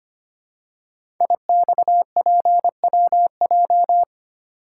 Morse code, heard as a keyed tone, about 25 wpm, 700 Hz. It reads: IXPWJ
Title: Code Groups - 5 Letters - 25wpm